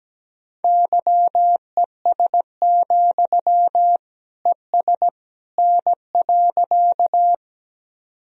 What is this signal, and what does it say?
Morse code, heard as a keyed tone, about 17 wpm, 700 Hz. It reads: YES, ES N.
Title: Sentences from Top 500 Words and Abbreviations 17wpm